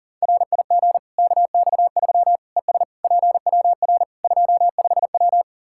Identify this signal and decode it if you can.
Morse code, heard as a keyed tone, about 40 wpm, 700 Hz. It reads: RIG KX3 ES PWR 25W
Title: Practice QSO - Exchange 2 - Info - 40wpm